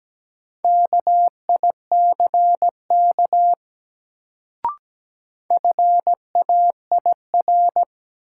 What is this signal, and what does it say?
Morse code, heard as a keyed tone, about 17 wpm, 700 Hz. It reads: KICK FAIR
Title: Sentences from Top 1300 Words - New Word Drill Alt - 17wpm